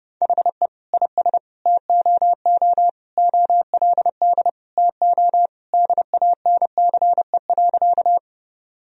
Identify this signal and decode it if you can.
Morse code, heard as a keyed tone, about 30 wpm, 700 Hz. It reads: HE IS TOO OLD TO DANCE.
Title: Sentences from Top 1000 Words - Review Part 4 - 30wpm